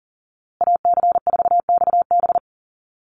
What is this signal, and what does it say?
Morse code, heard as a keyed tone, about 40 wpm, 700 Hz. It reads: AC4XB
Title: USA Call Signs Repeated 2x 40wpm